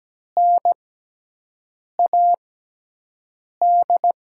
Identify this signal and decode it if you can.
Morse code, heard as a keyed tone, about 17 wpm, spaced as for 6 wpm, 700 Hz. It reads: NAD
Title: USA Call Signs - 6 Characters - 6x Character Spacing - 17wpm